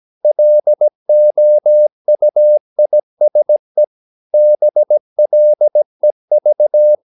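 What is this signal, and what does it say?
Morse code, heard as a keyed tone, about 17 wpm, 600 Hz. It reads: LOUISE BLEV
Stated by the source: Slow (QRS) Morse practice at 17 wpm, from CW med Gnister.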